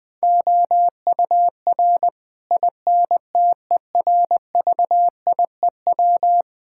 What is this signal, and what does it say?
Morse code, heard as a keyed tone, about 20 wpm, 700 Hz. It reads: OUR INTERVIEW
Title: Sentences from Top 1600 Words 20wpm